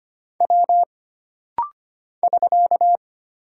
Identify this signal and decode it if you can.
Morse code, heard as a keyed tone, about 25 wpm, 700 Hz. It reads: W <SK>
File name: Single Character (Letter:Number:Punctuation:Prosign) - Rapid-Fire - 25wpm